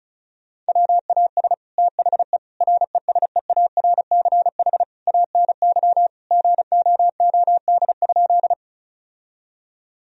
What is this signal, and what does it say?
Morse code, heard as a keyed tone, about 34 wpm, 700 Hz. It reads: WAS THE RESEARCH ANY GOOD?
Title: Sentences from Top 1000 Words - Review Part 4 - 35wpm